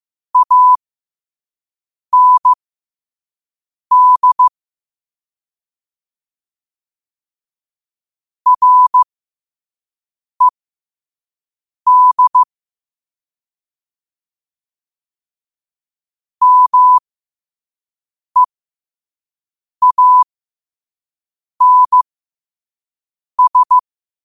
Morse code, heard as a keyed tone, about 15 wpm, spaced as for 5 wpm, 1000 Hz. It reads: AND RED MEANS
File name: qotdfiles / 5wpm